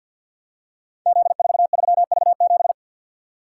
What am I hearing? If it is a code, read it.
Morse code, heard as a keyed tone, about 47 wpm, 700 Hz. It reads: Z43V7
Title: Code Groups - 5 Letter:Numbers - 50wpm